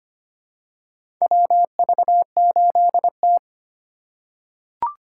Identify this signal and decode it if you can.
Morse code, heard as a keyed tone, about 25 wpm, 700 Hz. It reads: WV8T
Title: USA Call Signs - 4 Characters - 4x Character Spacing - 25wpm